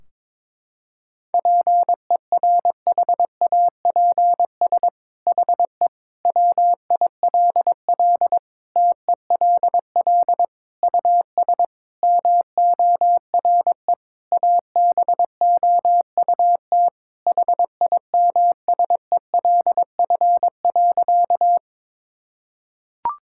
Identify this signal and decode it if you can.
Morse code, heard as a keyed tone, about 22 wpm, 700 Hz. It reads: PERHAPS HE WILL TELL US MORE ABOUT HIMSELF.
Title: Sentences from Top 1600 Words 22wpm